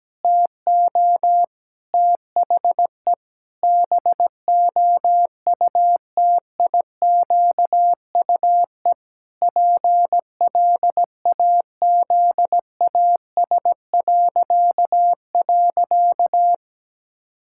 Morse code, heard as a keyed tone, about 17 wpm, 700 Hz. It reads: TO THE BOUTIQUE PLAZAS..
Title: Pangram Challenge v2 - 17wpm